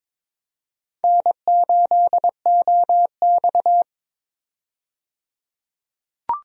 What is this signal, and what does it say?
Morse code, heard as a keyed tone, about 22 wpm, 700 Hz. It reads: N8OX
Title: USA Call Signs - 4 Characters - 22wpm